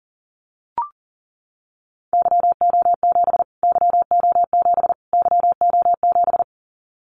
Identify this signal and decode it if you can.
Morse code, heard as a keyed tone, about 40 wpm, 700 Hz. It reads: YO7 YO7 YO7
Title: Callsign Prefixes for Top 25 DXCC - Repeated 3x - 40wpm